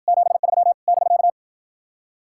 Morse code, heard as a keyed tone, about 61 wpm, 700 Hz. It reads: 73 <BK>